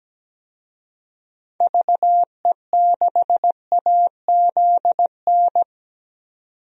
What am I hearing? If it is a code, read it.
Morse code, heard as a keyed tone, about 17 wpm, 700 Hz. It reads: VE6AZN
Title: DXCC-100 Call Signs - 6 Characters - 17wpm